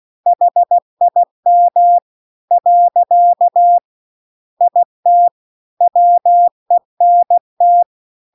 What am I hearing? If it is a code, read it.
Morse code, heard as a keyed tone, about 16 wpm, 700 Hz. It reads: HIM . IT WENT
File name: CW-The-Big-Twoheaded-River-2-16WPM